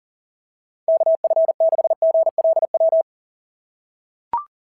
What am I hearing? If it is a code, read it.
Morse code, heard as a keyed tone, about 38 wpm, 660 Hz. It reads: KF6GLW
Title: SOTA QSO Practice 40wpm